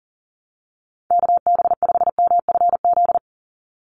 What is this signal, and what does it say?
Morse code, heard as a keyed tone, about 38 wpm, 700 Hz. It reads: KB5MFZ
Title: USA Call Signs 40wpm